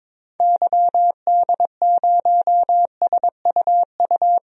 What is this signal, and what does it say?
Morse code, heard as a keyed tone, about 22 wpm, 700 Hz. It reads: YD0SUU
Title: DXCC-100 Call Signs - 6 Characters - Repeated 2x - 22wpm